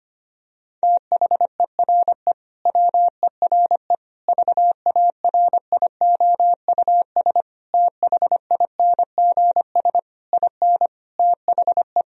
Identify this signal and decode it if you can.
Morse code, heard as a keyed tone, about 25 wpm, 700 Hz. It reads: THERE WERE VARIOUS THINGS IN THE
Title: Sentences from Top 1000 Words - Review Part 4 - 25wpm